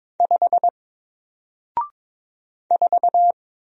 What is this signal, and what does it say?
Morse code, heard as a keyed tone, about 22 wpm, 700 Hz. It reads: 5 4
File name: Single-Letter-Number-22wpm